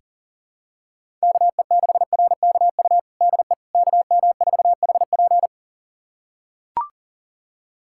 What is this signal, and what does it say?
Morse code, heard as a keyed tone, about 37 wpm, 700 Hz. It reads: KE6RKU DE KM4HP
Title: USA Callsign DE Callsign 40wpm